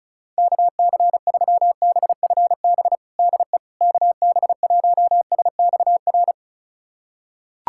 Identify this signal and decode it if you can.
Morse code, heard as a keyed tone, about 36 wpm, 700 Hz. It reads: KC3BFB DE KB1SXR
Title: USA Callsign DE Callsign - 6 Characters - 35wpm